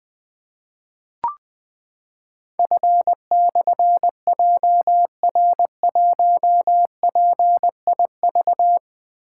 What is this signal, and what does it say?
Morse code, heard as a keyed tone, about 20 wpm, 700 Hz. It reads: F/JR1PIV
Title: Difficult Call Signs 20wpm